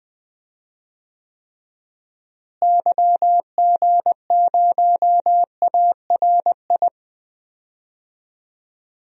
Morse code, heard as a keyed tone, about 20 wpm, 700 Hz. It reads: YG0ARI
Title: DXCC-100 Call Signs - 6 Characters - Repeated 3x - 20wpm